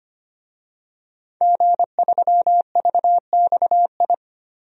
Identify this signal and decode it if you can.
Morse code, heard as a keyed tone, about 25 wpm, 700 Hz. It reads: G3VXI
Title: DXCC-100 Call Signs - 5 Characters - Repeated 2x - 25wpm